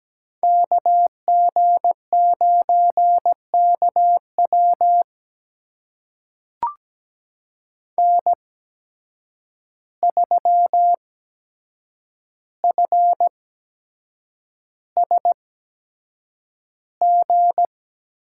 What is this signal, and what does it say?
Morse code, heard as a keyed tone, about 17 wpm, spaced as for 7 wpm, 700 Hz. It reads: KG9KW N3FSG
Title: USA Call Signs - 5 Characters - 8x Character Spacing - 17wpm